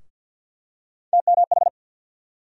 Morse code, heard as a keyed tone, about 50 wpm, 700 Hz. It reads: TMH
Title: USA Call Sign Postfixes - 3 Letters 50wpm